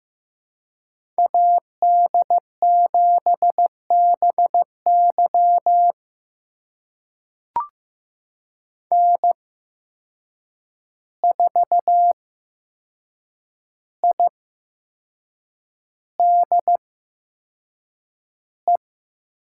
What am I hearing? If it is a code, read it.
Morse code, heard as a keyed tone, about 15 wpm, spaced as for 6 wpm, 700 Hz. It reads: AD7BY N4IDE
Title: USA Call Signs - 5 Characters - 8x Character Spacing - 15wpm